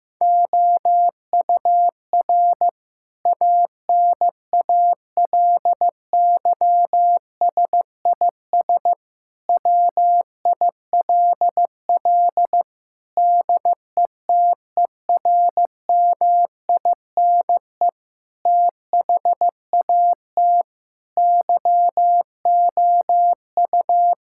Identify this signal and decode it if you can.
Morse code, heard as a keyed tone, about 15 wpm, 700 Hz. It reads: OUR ANALYSIS WILL DETERMINE THAT YOU
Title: Sentences from Top 1000 Words - Review Part 5 - 15wpm